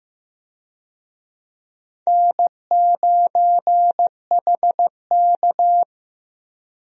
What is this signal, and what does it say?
Morse code, heard as a keyed tone, about 16 wpm, 700 Hz. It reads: N9HK
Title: USA Call Signs - 4 Characters - Repeated 3x - 15wpm